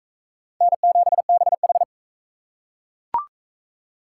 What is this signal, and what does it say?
Morse code, heard as a keyed tone, about 38 wpm, 700 Hz. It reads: N7BH